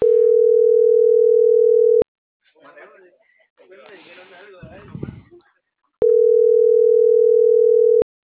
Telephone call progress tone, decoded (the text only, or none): ringback tone